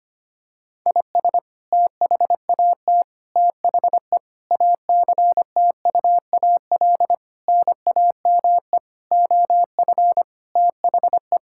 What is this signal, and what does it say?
Morse code, heard as a keyed tone, about 25 wpm, 700 Hz. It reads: IS THAT THE ACTUAL NAME OF THE
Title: Sentences from Top 1000 Words - Review Part 5 - 25wpm